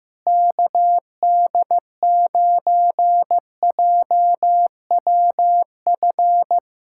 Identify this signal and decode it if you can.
Morse code, heard as a keyed tone, about 16 wpm, 700 Hz. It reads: KD9JWF
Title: USA Callsign DE Callsign - 6 Characters - 15wpm